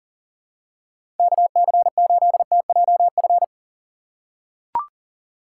Morse code, heard as a keyed tone, about 40 wpm, 700 Hz. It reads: KC8TJF
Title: USA Call Signs - 6 Characters - 3x Character Spacing - 40wpm